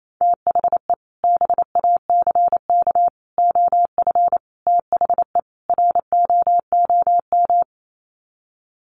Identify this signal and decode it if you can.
Morse code, heard as a keyed tone, about 28 wpm, 700 Hz. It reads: THE BACK OF THE ROOM
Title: Sets of 5 Words Encore 28wpm